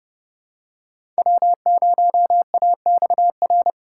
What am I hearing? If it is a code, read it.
Morse code, heard as a keyed tone, about 31 wpm, 700 Hz. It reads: W0AXR
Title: USA Call Signs - 5 Characters - 3x Character Spacing - 30wpm